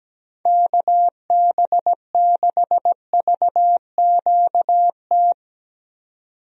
Morse code, heard as a keyed tone, about 17 wpm, 700 Hz. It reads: KB6VQT
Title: USA Call Signs - 6 Characters - 17wpm